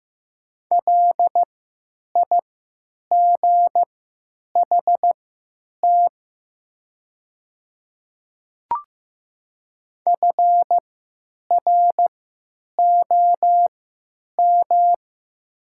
Morse code, heard as a keyed tone, about 15 wpm, spaced as for 9 wpm, 700 Hz. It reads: LIGHT FROM
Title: 3x Character Spacing - Top 200 Words - 15wpm